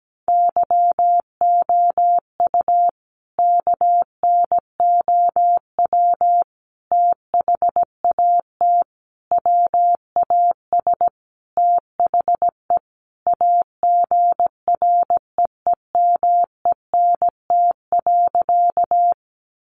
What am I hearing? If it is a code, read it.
Morse code, heard as a keyed tone, about 17 wpm, 700 Hz. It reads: YOU KNOW THAT WAS THE AGREEMENT.
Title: Sentences from Top 1500 Words 17wpm